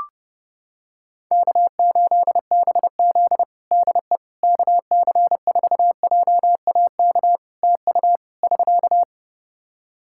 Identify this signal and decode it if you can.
Morse code, heard as a keyed tone, about 30 wpm, 700 Hz. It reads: K8BZ DE KC4JAK TU <SK>
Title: Practice QSO - Exchange 4 - Ending - 30wpm